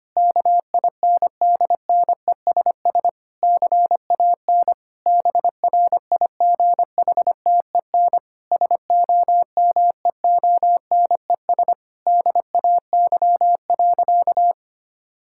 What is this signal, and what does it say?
Morse code, heard as a keyed tone, about 25 wpm, 700 Hz. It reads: KINDNESS CAN BRIGHTEN SOMEONES DAY.
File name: practice / 25wpm / Difficult Sentences with Hints - 25wpm